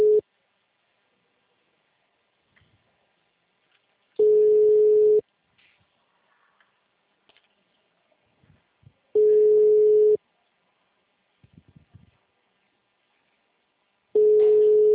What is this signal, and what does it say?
Ringback tone (Europe, 425 Hz).